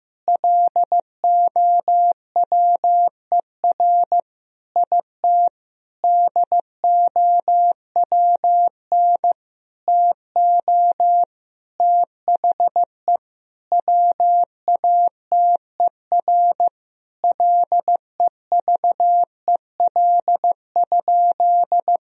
Morse code, heard as a keyed tone, about 16 wpm, 700 Hz. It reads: LOWER IT DOWN TO THE WATER LEVEL?
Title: Sentences from Top 1000 Words - Review Part 4 - 15wpm